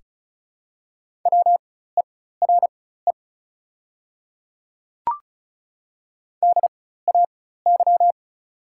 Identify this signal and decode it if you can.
Morse code, heard as a keyed tone, about 36 wpm, spaced as for 16 wpm, 700 Hz. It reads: WERE DAY
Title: 4x Character Spacing - Top 100 Words - 35wpm